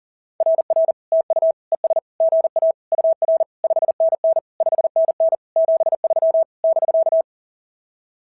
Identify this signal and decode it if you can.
Morse code, heard as a keyed tone, about 37 wpm, 690 Hz. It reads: RR TU ES GA UR 5NN 5NN 73 <BK>
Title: SOTA QSO Practice 40wpm